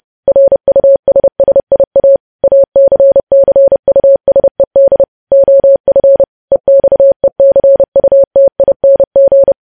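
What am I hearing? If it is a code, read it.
Morse code, heard as a keyed tone, about 30 wpm, 550 Hz. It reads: RUSSIA ACCUSED OF EXECUTING
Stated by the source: Morse code news headlines at 30.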